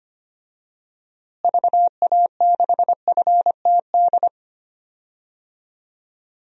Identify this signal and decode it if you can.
Morse code, heard as a keyed tone, about 25 wpm, 700 Hz. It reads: VA6FTD
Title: DXCC-100 Call Signs - 6 Characters - Repeated 3x - 25wpm